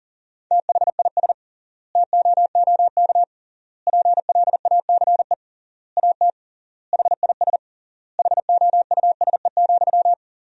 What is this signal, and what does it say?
Morse code, heard as a keyed tone, about 40 wpm, 700 Hz. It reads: THIS TOOK PLACE AT HIS HOUSE,
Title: 3x Word-Spacing - Sentences from Top 300 Words - 40wpm